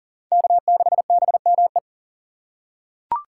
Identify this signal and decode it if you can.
Morse code, heard as a keyed tone, about 40 wpm, 700 Hz. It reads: K6BME
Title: USA Call Signs - 5 Characters - 3x Character Spacing - 40wpm